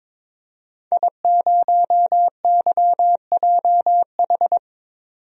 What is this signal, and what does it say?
Morse code, heard as a keyed tone, about 22 wpm, 700 Hz. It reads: I0YJH
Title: DXCC-100 Call Signs - 5 Characters - 22wpm